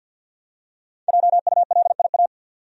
Morse code, heard as a keyed tone, about 52 wpm, 700 Hz. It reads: JULIA
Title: USA Common Names - Max 5 Letters - 50wpm